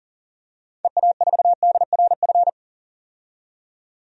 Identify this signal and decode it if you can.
Morse code, heard as a keyed tone, about 40 wpm, 700 Hz. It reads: EA4DRF
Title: DXCC-100 Call Signs - 6 Characters - 40wpm